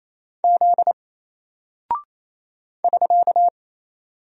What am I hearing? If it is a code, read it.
Morse code, heard as a keyed tone, about 27 wpm, 700 Hz. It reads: Z <SK>
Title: Single Character (Letter:Number:Punctuation:Prosign) 28wpm